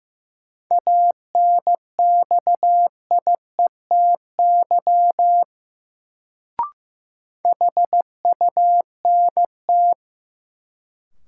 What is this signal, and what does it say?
Morse code, heard as a keyed tone, about 15 wpm, 700 Hz. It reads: ANXIETY HUNT
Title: Sentences from Top 1600 Words - New Word Drill - 15wpm